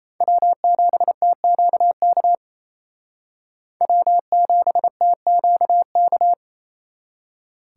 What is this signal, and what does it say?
Morse code, heard as a keyed tone, about 34 wpm, 700 Hz. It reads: W7TQK W7TQK
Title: Speed Racing - USA 1x3 Call Signs 28wpm